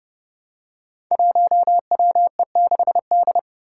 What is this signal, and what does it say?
Morse code, heard as a keyed tone, about 30 wpm, 700 Hz. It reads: 1WE6D
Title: Five Challenging Letter-Number Combinations 30wpm